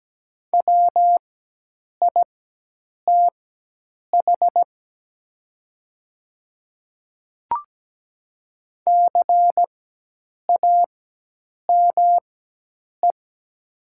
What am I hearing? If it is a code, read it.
Morse code, heard as a keyed tone, about 17 wpm, spaced as for 8 wpm, 700 Hz. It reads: WITH CAME
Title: 4x Character Spacing - Top 200 Words - 17wpm